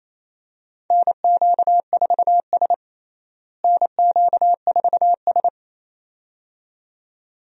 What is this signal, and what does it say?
Morse code, heard as a keyed tone, about 28 wpm, 700 Hz. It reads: NQ4S NQ4S
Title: USA Call Signs - 4 Characters - Repeated 2x - 28wpm